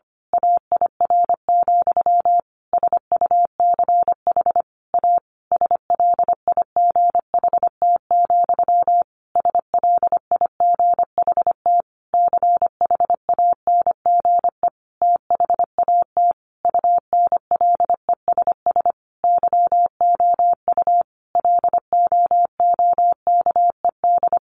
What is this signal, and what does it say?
Morse code, heard as a keyed tone, about 25 wpm, 700 Hz. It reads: AIR, SUCH A SLIGHT, SLIGHT CHANGE THAT UNLESS YOU LOOKED
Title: A Sound Of Thunder 25wpm